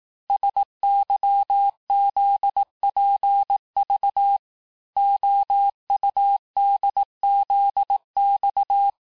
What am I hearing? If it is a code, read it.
Morse code, heard as a keyed tone, about 18 wpm, 800 Hz. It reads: SYZPV OUDZX
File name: cours-du-lundi files / cours-1852-texte-chiffre-vitesse-17MPM